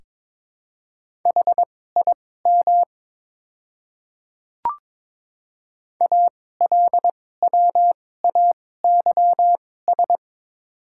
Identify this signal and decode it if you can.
Morse code, heard as a keyed tone, about 23 wpm, spaced as for 16 wpm, 700 Hz. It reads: HIM ALWAYS
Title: 2x Character Spacing - Top 300 Words - 22wpm